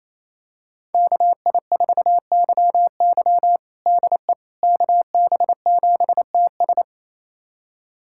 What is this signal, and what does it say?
Morse code, heard as a keyed tone, about 28 wpm, 700 Hz. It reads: KI4YY DE KB7TS
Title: USA Callsign DE Callsign - 5 Characters - 28wpm